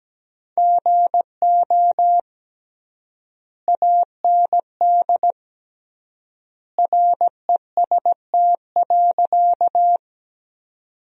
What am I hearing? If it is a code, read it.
Morse code, heard as a keyed tone, about 17 wpm, 700 Hz. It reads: GO AND REST.
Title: 3x Word-Spacing - Sentences from Top 500 Words - 17wpm